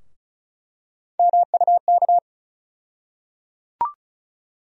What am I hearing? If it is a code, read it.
Morse code, heard as a keyed tone, about 36 wpm, 700 Hz. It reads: MUK
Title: USA Call Sign Postfixes - 3 Letters 35wpm